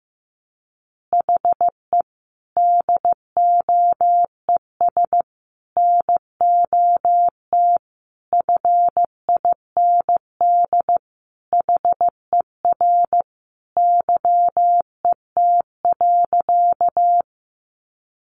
Morse code, heard as a keyed tone, about 15 wpm, 700 Hz. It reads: HE DOES NOT FIND HER YET.
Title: Easy Phrases 15wpm